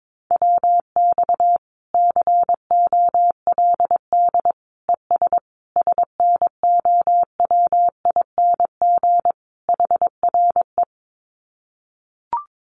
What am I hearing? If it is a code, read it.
Morse code, heard as a keyed tone, about 22 wpm, 700 Hz. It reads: WX COLD ES SNOWING HRE
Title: Common QSO Phrases 22wpm